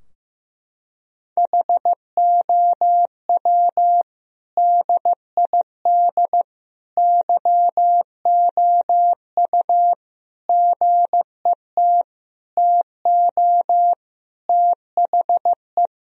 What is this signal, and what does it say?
Morse code, heard as a keyed tone, about 15 wpm, 700 Hz. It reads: HOW DID YOU GET TO THE
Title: Sentences from Top 1000 Words - Review Part 3 - 15wpm